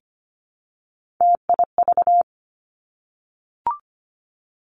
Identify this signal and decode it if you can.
Morse code, heard as a keyed tone, about 25 wpm, 700 Hz. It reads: TIV
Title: Three Challenging Letter-Number Combinations 25wpm